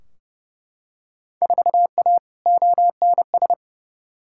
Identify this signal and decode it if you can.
Morse code, heard as a keyed tone, about 30 wpm, 700 Hz. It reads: 4A ONS
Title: Field Day Exchanges - 30wpm